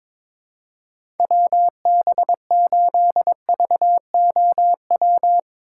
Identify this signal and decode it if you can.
Morse code, heard as a keyed tone, about 22 wpm, 700 Hz. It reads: WB8VOW
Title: USA Call Signs - 6 Characters - 2x Character Spacing - 22wpm